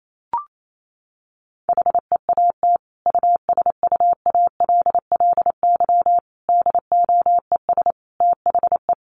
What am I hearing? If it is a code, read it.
Morse code, heard as a keyed tone, about 28 wpm, 700 Hz. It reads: HEAT USUALLY DOES THE
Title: Sentences from Top 1500 Words 28wpm